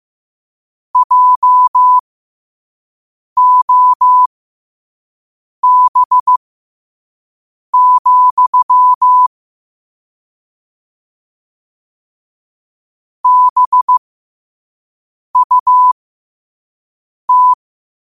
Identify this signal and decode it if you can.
Morse code, heard as a keyed tone, about 15 wpm, spaced as for 5 wpm, 1000 Hz. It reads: JOB, BUT
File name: qotdfiles / 5wpm